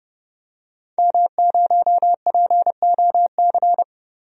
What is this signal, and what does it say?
Morse code, heard as a keyed tone, about 32 wpm, 700 Hz. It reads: M0POC
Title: DXCC-100 Call Signs - 5 Characters - 30wpm